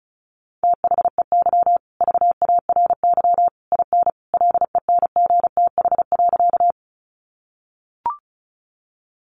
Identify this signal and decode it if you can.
Morse code, heard as a keyed tone, about 34 wpm, 700 Hz. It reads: THEY VARY IN LENGTH.
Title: Sentences from Top 1200 Words 35wpm